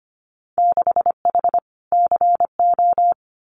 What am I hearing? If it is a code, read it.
Morse code, heard as a keyed tone, about 25 wpm, 700 Hz. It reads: 6H CO
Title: Winter Field Day Exchanges Encore 25wpm